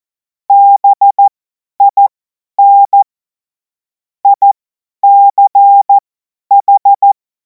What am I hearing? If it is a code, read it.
Morse code, heard as a keyed tone, about 14 wpm, spaced as for 10 wpm, 800 Hz. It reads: BIN ICH
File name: Morsekurs / Witze / Witz 21. April 2026 50 F70